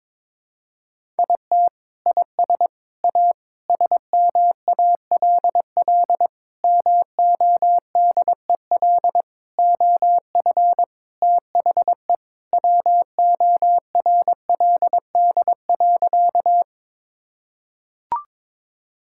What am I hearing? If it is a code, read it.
Morse code, heard as a keyed tone, about 22 wpm, 700 Hz. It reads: IT IS A SMALL MODEL OF THE WORLD.
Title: Sentences from Top 1000 Words - Review Part 5 - 22wpm